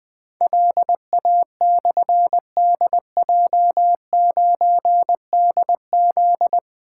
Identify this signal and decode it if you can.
Morse code, heard as a keyed tone, about 21 wpm, 700 Hz. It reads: LA/DJ9DZ
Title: Difficult Call Signs 20wpm